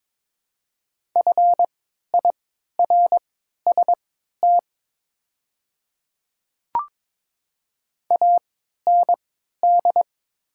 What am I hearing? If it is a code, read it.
Morse code, heard as a keyed tone, about 22 wpm, spaced as for 13 wpm, 700 Hz. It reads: FIRST AND
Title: 3x Character Spacing - Top 100 Words - 22wpm